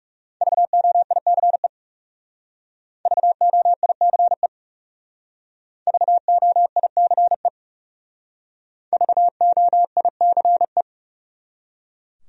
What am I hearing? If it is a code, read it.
Morse code, heard as a keyed tone, about 36 wpm, 700 Hz. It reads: VOICE VOICE VOICE VOICE
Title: Speed-Racing Top 500 Words 30wpm